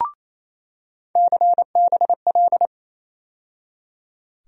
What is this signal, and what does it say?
Morse code, heard as a keyed tone, about 28 wpm, 700 Hz. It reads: CBL
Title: USA Call Sign Postfixes - 3 Letters 28wpm